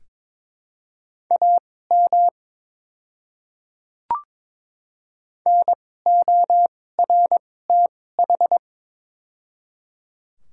Morse code, heard as a keyed tone, about 22 wpm, spaced as for 16 wpm, 700 Hz. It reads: AM NORTH
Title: 2x Character Spacing - Top 500 Words - 22wpm